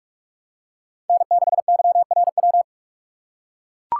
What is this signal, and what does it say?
Morse code, heard as a keyed tone, about 41 wpm, 700 Hz. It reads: N6YRW